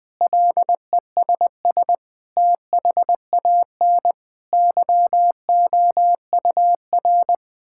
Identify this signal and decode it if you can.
Morse code, heard as a keyed tone, about 20 wpm, 700 Hz. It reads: LESS THAN YOUR
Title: Sentences from Top 1000 Words - Review Part 5 - 20wpm